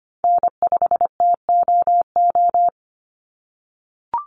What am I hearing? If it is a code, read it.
Morse code, heard as a keyed tone, about 25 wpm, 700 Hz. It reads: N5TOO
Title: Call Signs - 25wpm